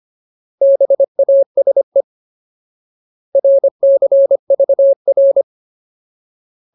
Morse code, heard as a keyed tone, about 25 wpm, 550 Hz. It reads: BASE RCVR
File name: Ultimate Word List / 25WPM Ultimate